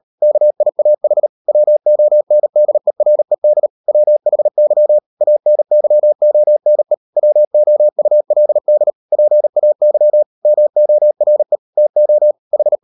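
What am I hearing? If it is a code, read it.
Morse code, heard as a keyed tone, about 38 wpm, 600 Hz. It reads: KIAH WONDERED WHY ANYONE WOULD PAY MORE TO H
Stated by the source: Morse code practice at 38 WPM, sending Ken Follett's novel Never in English, from CW med Gnister.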